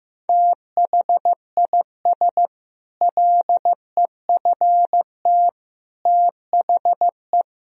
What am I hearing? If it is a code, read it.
Morse code, heard as a keyed tone, about 15 wpm, 700 Hz. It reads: THIS LEFT THE
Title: Sentences from Top 1000 Words - Review Part 4 - 15wpm